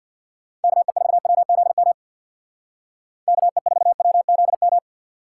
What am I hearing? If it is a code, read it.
Morse code, heard as a keyed tone, about 64 wpm, 700 Hz. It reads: KE4WZM KE4WZM